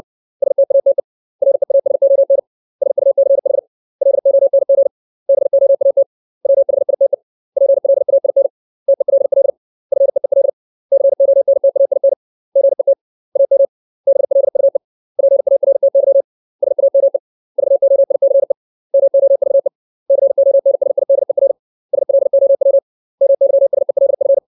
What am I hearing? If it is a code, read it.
Morse code, heard as a keyed tone, about 60 wpm, 550 Hz. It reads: STATE REASON SUCH LONG BOAT WHITE PLAIN TELL FEEL CONTAIN GET AM BLUE PARTY SAME VOICE MOVE CONSIDER SLOW MYSELF
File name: 632 words / 632 Words 60WPM